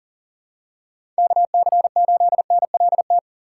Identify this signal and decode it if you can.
Morse code, heard as a keyed tone, about 40 wpm, 700 Hz. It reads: KC8NLT
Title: USA Call Signs - 6 Characters - 6x Character Spacing - 40wpm